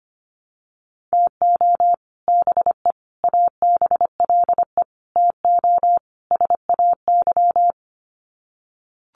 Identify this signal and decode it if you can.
Morse code, heard as a keyed tone, about 25 wpm, 700 Hz. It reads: TO BE ABLE TO SAY
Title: Sets of 5 Words Encore2 25wpm